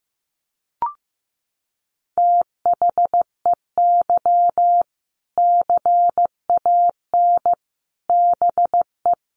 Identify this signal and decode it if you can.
Morse code, heard as a keyed tone, about 15 wpm, 700 Hz. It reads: THEY CAN BE
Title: Sentences from Top 1500 Words 15wpm